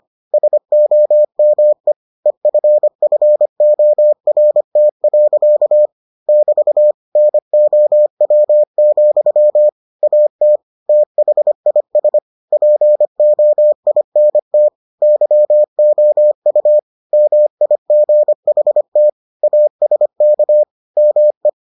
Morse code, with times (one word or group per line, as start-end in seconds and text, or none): 0.3-1.9 SOME
2.3-5.9 EFFORT.
6.3-6.9 <BT>
7.2-9.7 NOW,
10.0-10.6 AT
10.9-12.2 THIS
12.5-14.7 POINT
15.0-16.8 YOU
17.1-19.1 MIGHT
19.4-20.6 ASK
21.0-21.5 ME